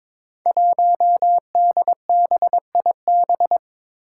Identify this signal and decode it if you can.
Morse code, heard as a keyed tone, about 22 wpm, 700 Hz. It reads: 1DBIB
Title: Five Challenging Letter-Number Combinations 22wpm